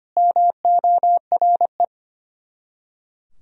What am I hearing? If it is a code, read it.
Morse code, heard as a keyed tone, about 25 wpm, 700 Hz. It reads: MORE
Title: Top 100 Words Repeated 25wpm